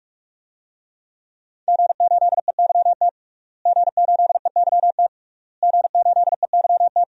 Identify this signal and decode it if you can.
Morse code, heard as a keyed tone, about 46 wpm, 700 Hz. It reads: G8EYT G8EYT G8EYT
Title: DXCC-100 Call Signs - 5 Characters - Repeated 3x - 45wpm